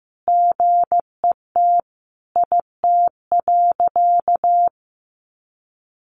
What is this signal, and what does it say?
Morse code, heard as a keyed tone, about 15 wpm, 700 Hz. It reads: GET IT.
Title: Sentences from Top 1500 Words 15wpm